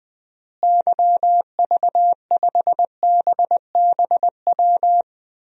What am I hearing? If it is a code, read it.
Morse code, heard as a keyed tone, about 21 wpm, 700 Hz. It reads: YV5BBW
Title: DXCC-100 Call Signs - 6 Characters - Repeated 3x - 20wpm